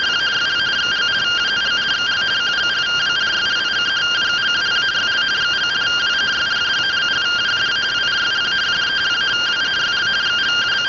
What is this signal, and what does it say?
Two-tone teleprinter signal, about 170 Hz shift.